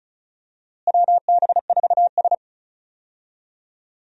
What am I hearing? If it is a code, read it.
Morse code, heard as a keyed tone, about 36 wpm, 700 Hz. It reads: WB4S
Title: USA Call Signs - 4 Characters - Repeated 2x - 35wpm